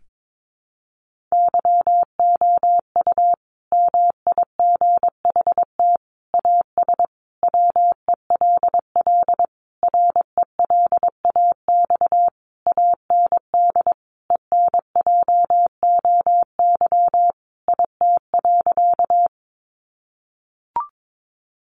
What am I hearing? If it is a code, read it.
Morse code, heard as a keyed tone, about 22 wpm, 700 Hz. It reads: YOU MIGHT AS WELL RELAX AND ENJOY IT.
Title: Sentences from Top 1400 Words 22wpm